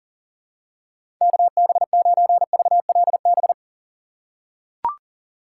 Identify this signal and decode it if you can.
Morse code, heard as a keyed tone, about 37 wpm, 700 Hz. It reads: KB9VLB
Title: USA Call Signs - 3x Character Spacing - 40wpm